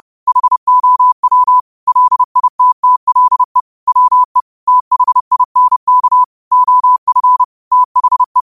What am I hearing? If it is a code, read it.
Morse code, heard as a keyed tone, about 30 wpm, 1000 Hz. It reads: HOW LITTLE WE THINK OF THE